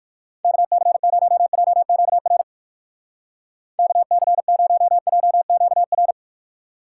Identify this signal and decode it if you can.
Morse code, heard as a keyed tone, about 56 wpm, 700 Hz. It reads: KC0JQR KC0JQR
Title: Speed Racing - USA 2x3 Call Signs 45wpm